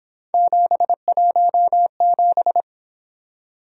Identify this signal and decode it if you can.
Morse code, heard as a keyed tone, about 28 wpm, 700 Hz. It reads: 717